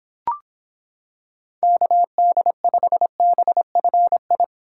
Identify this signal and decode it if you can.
Morse code, heard as a keyed tone, about 26 wpm, 700 Hz. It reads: KD5BFI